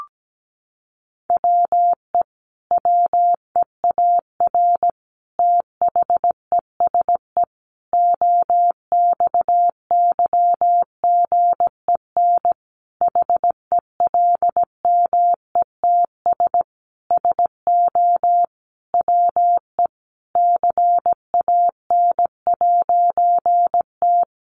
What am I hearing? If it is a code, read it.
Morse code, heard as a keyed tone, about 17 wpm, 700 Hz. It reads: WE WEAR THESE OXYGEN HELMETS SO WE CAN'T
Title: A Sound Of Thunder 17wpm